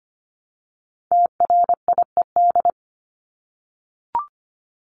Morse code, heard as a keyed tone, about 25 wpm, 700 Hz. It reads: TRIED
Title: ED Suffix - Top 100 Words with Base Words 25wpm